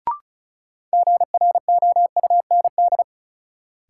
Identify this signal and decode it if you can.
Morse code, heard as a keyed tone, about 36 wpm, 700 Hz. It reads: GROUND
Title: Sets of 3 Words - New Word Drill - 35wpm